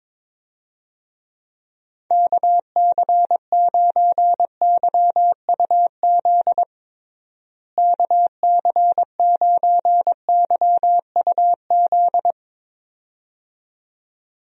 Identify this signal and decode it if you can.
Morse code, heard as a keyed tone, about 22 wpm, 700 Hz. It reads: KC9YUZ KC9YUZ
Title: USA Call Signs - 6 Characters - Repeated 2x - 22wpm